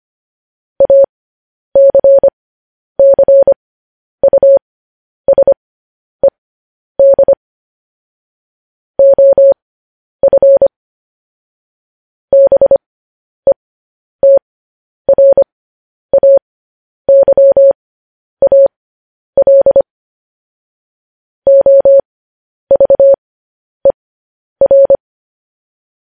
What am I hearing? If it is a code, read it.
Morse code, heard as a keyed tone, about 25 wpm, spaced as for 10 wpm, 550 Hz. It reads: ACCUSED OF BETRAYAL OVER
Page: news-headlines-in-morse-code-at-10-wpm